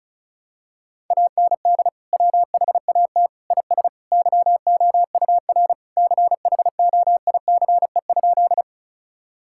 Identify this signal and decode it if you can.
Morse code, heard as a keyed tone, about 36 wpm, 700 Hz. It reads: AND WHAT IS YOUR CHOICE?
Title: Sentences from Top 1000 Words - Review Part 6 - 35wpm